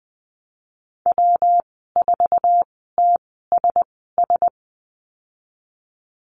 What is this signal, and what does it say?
Morse code, heard as a keyed tone, about 20 wpm, 700 Hz. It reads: W 4 T S S
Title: 154 Lesson 25g - Letter p - Call Signs 20wpm